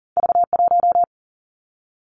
Morse code, heard as a keyed tone, about 42 wpm, 700 Hz. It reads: V 1
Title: Two Challenging Letter-Number Combinations 40wpm